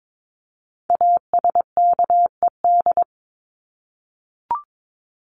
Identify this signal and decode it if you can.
Morse code, heard as a keyed tone, about 22 wpm, 700 Hz. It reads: ASKED
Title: ED Suffix - Top 100 Words with Base Words 22wpm